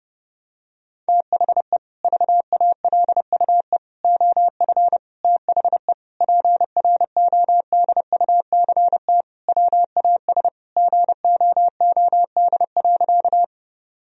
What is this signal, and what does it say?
Morse code, heard as a keyed tone, about 32 wpm, 700 Hz. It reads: THE VALUE OF THE PRODUCT WAS GOOD.
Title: Sentences from Top 1000 Words - Review Part 3 - 30wpm